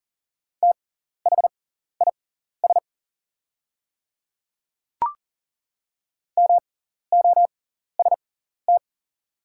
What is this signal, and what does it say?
Morse code, heard as a keyed tone, about 40 wpm, spaced as for 14 wpm, 700 Hz. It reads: THIS MOST
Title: 6x Character Spacing - Top 100 Words - 40wpm